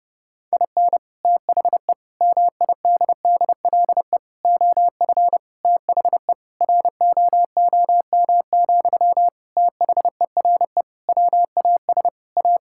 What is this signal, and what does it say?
Morse code, heard as a keyed tone, about 30 wpm, 700 Hz. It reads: IN THE MIDDLE OF THE ROOM, THERE WAS A
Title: Sentences from Top 1000 Words - Review Part 6 - 30wpm